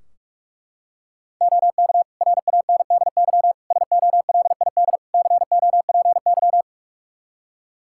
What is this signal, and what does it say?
Morse code, heard as a keyed tone, about 42 wpm, 700 Hz. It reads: OK RANDY SOLID COPY
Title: Practice QSO - Exchange 3 - More Info - 45wpm